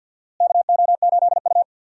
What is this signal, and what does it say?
Morse code, heard as a keyed tone, about 49 wpm, 700 Hz. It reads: KO8U
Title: USA Call Signs - 4 Characters - Repeated 3x - 50wpm